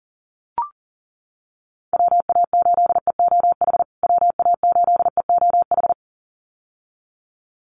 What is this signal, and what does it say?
Morse code, heard as a keyed tone, about 40 wpm, 700 Hz. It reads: WA8EOH WA8EOH
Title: USA Call Signs Repeated 2x 40wpm